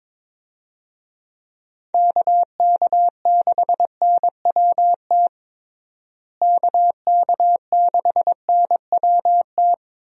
Morse code, heard as a keyed tone, about 21 wpm, 700 Hz. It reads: KK6NWT KK6NWT
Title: USA Call Signs - 6 Characters - Repeated 2x - 22wpm